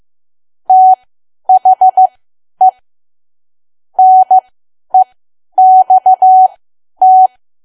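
Morse code, heard as a keyed tone, about 15 wpm, spaced as for 10 wpm, 750 Hz. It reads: THE NEXT
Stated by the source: Morse learning practice at 10 wpm.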